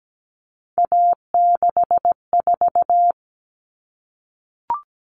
Morse code, heard as a keyed tone, about 17 wpm, 700 Hz. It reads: A64
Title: Common Cut Numbers - 3-Digit 17wpm